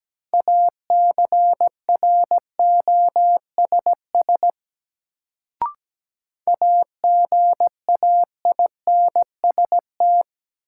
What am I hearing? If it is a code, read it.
Morse code, heard as a keyed tone, about 17 wpm, 700 Hz. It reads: ACROSS AGAINST
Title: Sets of 2 Words - New Word Drill - 17wpm